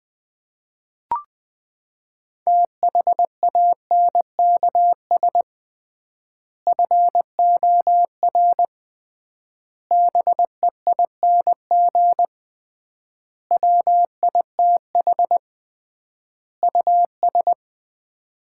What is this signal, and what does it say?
Morse code, heard as a keyed tone, about 20 wpm, 700 Hz. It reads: THANKS FOR BEING WITH US
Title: 3x Word-Spacing - Sets of 5 Words - 20wpm